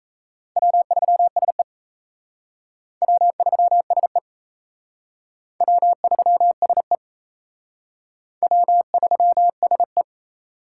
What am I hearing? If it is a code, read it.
Morse code, heard as a keyed tone, about 37 wpm, 700 Hz. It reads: W3SE W3SE W3SE W3SE